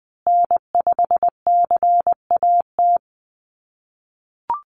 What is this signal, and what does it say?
Morse code, heard as a keyed tone, about 20 wpm, 700 Hz. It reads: N5CAT
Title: USA Call Signs Repeated 2x 20wpm